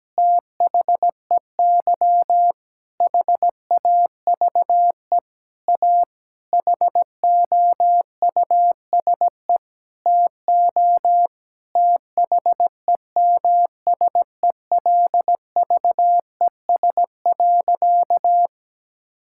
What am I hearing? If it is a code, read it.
Morse code, heard as a keyed tone, about 17 wpm, 700 Hz. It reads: THEY HAVE A HOUSE TO THEMSELVES.
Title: Sentences from Top 1000 Words - Review Part 5 - 17wpm